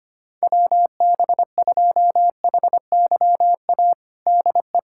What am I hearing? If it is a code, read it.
Morse code, heard as a keyed tone, about 25 wpm, 700 Hz. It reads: WB2HYA DE
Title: USA Callsign DE Callsign - 6 Characters - 25wpm